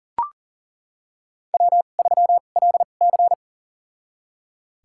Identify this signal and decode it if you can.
Morse code, heard as a keyed tone, about 40 wpm, spaced as for 29 wpm, 700 Hz. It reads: W3LC
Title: USA Call Signs - 4 Characters - 2x Character Spacing - 40wpm